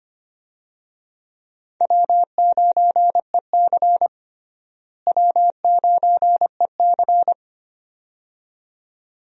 Morse code, heard as a keyed tone, about 25 wpm, 700 Hz. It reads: W9EC W9EC
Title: USA Call Signs - 4 Characters - Repeated 2x - 25wpm